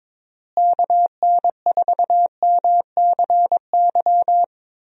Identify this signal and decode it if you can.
Morse code, heard as a keyed tone, about 22 wpm, 700 Hz. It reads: KN4MCY
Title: USA Call Signs - 6 Characters - Repeated 2x - 22wpm